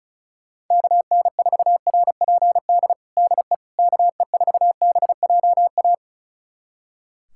Morse code, heard as a keyed tone, about 34 wpm, 700 Hz. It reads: KN4RPD DE KE4BJA
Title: USA Callsign DE Callsign - 6 Characters - 35wpm